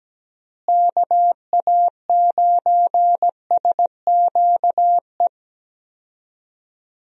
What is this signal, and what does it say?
Morse code, heard as a keyed tone, about 17 wpm, 700 Hz. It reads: KA9SQE
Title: USA Call Signs - 6 Characters - Repeated 2x - 17wpm